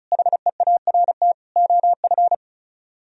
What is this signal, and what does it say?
Morse code, heard as a keyed tone, about 46 wpm, 700 Hz. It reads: HEART OF